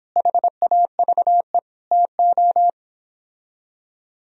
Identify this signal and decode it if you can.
Morse code, heard as a keyed tone, about 29 wpm, 700 Hz. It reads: HAVE TO